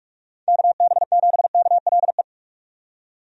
Morse code, heard as a keyed tone, about 46 wpm, 700 Hz. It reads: KB7KLE
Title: USA Call Signs - 6 Characters - 45wpm